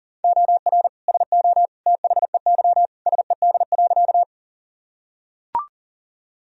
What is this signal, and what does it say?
Morse code, heard as a keyed tone, about 37 wpm, 700 Hz. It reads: OR SO THEY SED.
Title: Sentences from Top 100 Words and Abbreviations 40wpm